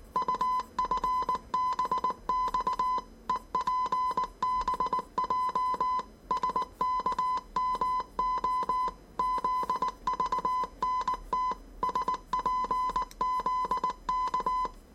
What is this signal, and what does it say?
Morse code, heard as a keyed tone, about 19 wpm, 1010 Hz. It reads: UFBX EPBJ SKMO ZVNT SPZK
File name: 业余无线电 / ●未分类 / 字码80